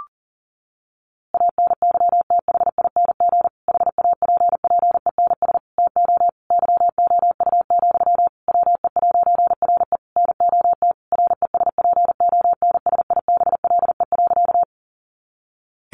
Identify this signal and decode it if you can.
Morse code, heard as a keyed tone, about 40 wpm, 700 Hz. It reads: ANYTHING HAPPENS TO YOU, WE'RE NOT RESPONSIBLE.
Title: A Sound Of Thunder 40wpm